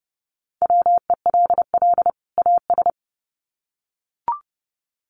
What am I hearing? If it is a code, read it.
Morse code, heard as a keyed tone, about 30 wpm, 700 Hz. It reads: WELL AS
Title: Sets of 2 Words 30wpm